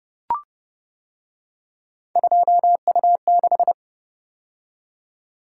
Morse code, heard as a keyed tone, about 29 wpm, 700 Hz. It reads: 2U6